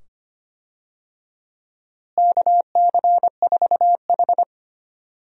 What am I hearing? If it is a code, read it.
Morse code, heard as a keyed tone, about 25 wpm, 700 Hz. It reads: KC4H
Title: USA Call Signs - 4 Characters - Repeated 2x - 25wpm